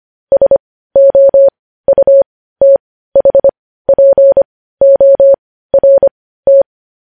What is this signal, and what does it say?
Morse code, heard as a keyed tone, about 25 wpm, spaced as for 15 wpm, 550 Hz. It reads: SOUTHPORT
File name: podcast / News headlines-1x-25x15